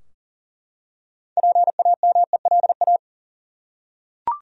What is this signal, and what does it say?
Morse code, heard as a keyed tone, about 40 wpm, 700 Hz. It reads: PAMELA
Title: USA Common Names - 40wpm